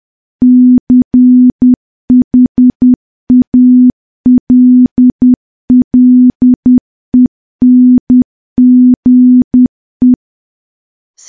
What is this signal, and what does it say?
Morse code, heard as a keyed tone, about 10 wpm, 250 Hz. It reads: CHALLENGE
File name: morse / morse-training-10WPM